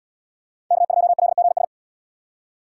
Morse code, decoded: K2LGS

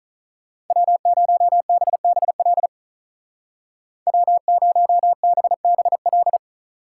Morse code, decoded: W0BBL W0BBL

